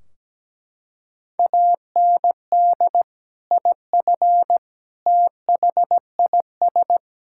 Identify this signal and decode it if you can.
Morse code, heard as a keyed tone, about 17 wpm, 700 Hz. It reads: AND IF THIS
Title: Sentences from Top 1600 Words 17wpm